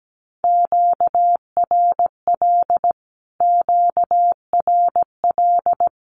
Morse code, heard as a keyed tone, about 17 wpm, 700 Hz. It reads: QRL QRL
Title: Q-Codes Repeated 3x 17wpm